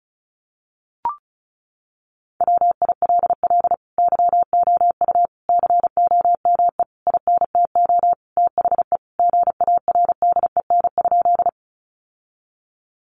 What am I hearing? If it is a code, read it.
Morse code, heard as a keyed tone, about 34 wpm, 700 Hz. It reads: WILL YOU COME INTO THE GARDEN?
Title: Sentences from Top 700 Words 35wpm